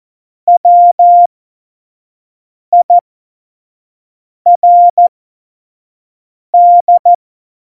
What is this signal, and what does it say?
Morse code, heard as a keyed tone, about 14 wpm, spaced as for 5 wpm, 700 Hz. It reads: WIRD
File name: Morsekurs / Witze / Witz 11. September 2018, 25 F70